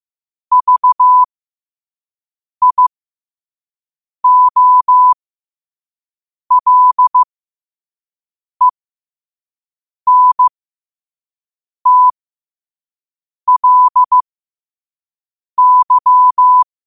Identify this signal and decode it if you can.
Morse code, heard as a keyed tone, about 15 wpm, spaced as for 5 wpm, 1000 Hz. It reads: VIOLENTLY
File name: qotdfiles / 5wpm